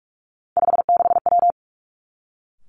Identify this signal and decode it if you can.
Morse code, heard as a keyed tone, about 46 wpm, 700 Hz. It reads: 56W
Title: Three Challenging Letter-Number Combinations 45wpm